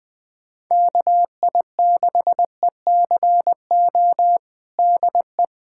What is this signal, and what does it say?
Morse code, heard as a keyed tone, about 20 wpm, 700 Hz. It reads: KI6ECO DE
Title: USA Callsign DE Callsign - 6 Characters - 20wpm